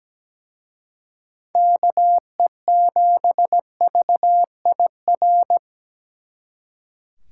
Morse code, heard as a keyed tone, about 20 wpm, 700 Hz. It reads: KE7VIR
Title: Speed Racing - USA 2x3 Call Signs 17wpm